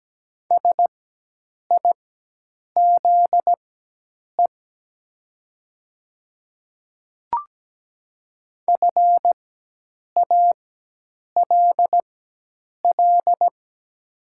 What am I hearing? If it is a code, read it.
Morse code, heard as a keyed tone, about 17 wpm, spaced as for 8 wpm, 700 Hz. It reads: SIZE FALL
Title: 4x Character Spacing - Top 500 Words - 17wpm